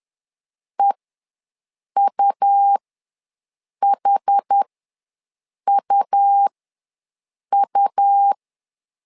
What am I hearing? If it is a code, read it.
Morse code, heard as a keyed tone, about 10 wpm, spaced as for 6 wpm, 800 Hz. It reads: EUHUU
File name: LEZIONI / 2 UHDN / 2 UH ws 10wpm 2029